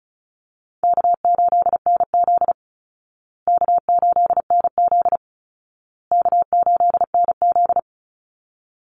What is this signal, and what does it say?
Morse code, heard as a keyed tone, about 34 wpm, 700 Hz. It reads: K8NZ K8NZ K8NZ
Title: Easy Call Signs Repeated 3x - 35wpm